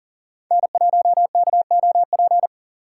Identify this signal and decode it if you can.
Morse code, heard as a keyed tone, about 40 wpm, 700 Hz. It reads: N1KOP
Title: USA Call Signs - 5 Characters - 3x Character Spacing - 40wpm